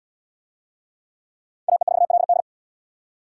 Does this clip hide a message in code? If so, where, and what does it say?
Yes: Morse code from 1.7 to 2.4 s, heard as a keyed tone, about 71 wpm, 700 Hz. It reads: AE4LD